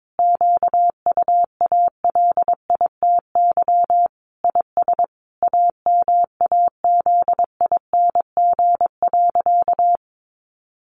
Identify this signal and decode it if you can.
Morse code, heard as a keyed tone, about 22 wpm, 700 Hz. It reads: QUALITY IS AMAZING.
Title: Sentences from Top 1500 Words 22wpm